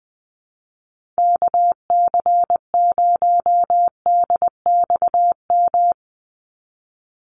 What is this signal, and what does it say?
Morse code, heard as a keyed tone, about 20 wpm, 700 Hz. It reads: KC0DXM
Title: USA Call Signs 20wpm